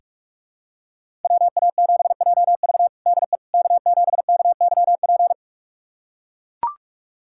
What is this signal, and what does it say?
Morse code, heard as a keyed tone, about 41 wpm, 700 Hz. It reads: WA7JV DE K7KYP
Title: USA Callsign DE Callsign - 5 Characters - 45wpm